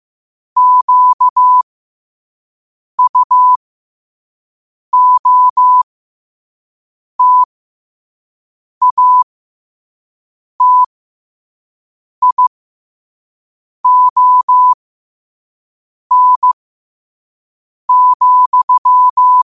Morse code, heard as a keyed tone, about 15 wpm, spaced as for 5 wpm, 1000 Hz. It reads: QUOTATION,